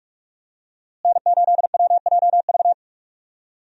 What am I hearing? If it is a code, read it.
Morse code, heard as a keyed tone, about 46 wpm, 700 Hz. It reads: N8WJV
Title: USA Call Signs - 5 Characters - Repeated 3x - 45wpm